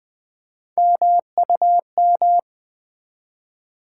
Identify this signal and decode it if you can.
Morse code, heard as a keyed tone, about 20 wpm, 700 Hz. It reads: MUM
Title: Build A Word - 1 Letter At A Time - 3-Letter Words - 20wpm